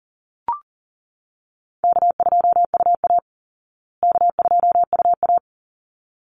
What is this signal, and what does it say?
Morse code, heard as a keyed tone, about 39 wpm, 700 Hz. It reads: K2UA K2UA
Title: Call Signs Repeated 3x - 40wpm